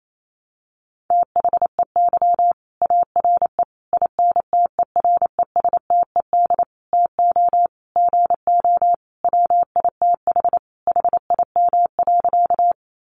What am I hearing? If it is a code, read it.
Morse code, heard as a keyed tone, about 28 wpm, 700 Hz. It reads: THEY ARE INTERESTED TO GO WITH HIM.
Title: Sentences from Top 900 Words 28wpm